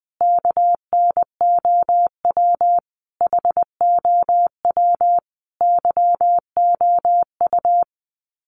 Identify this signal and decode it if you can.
Morse code, heard as a keyed tone, about 20 wpm, 700 Hz. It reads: KNOW HOW YOU
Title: Sets of 5 Words Encore 20wpm